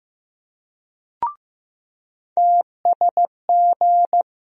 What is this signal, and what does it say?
Morse code, heard as a keyed tone, about 16 wpm, 700 Hz. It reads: TSG
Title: USA Call Sign Postfixes - 3 Letters 15wpm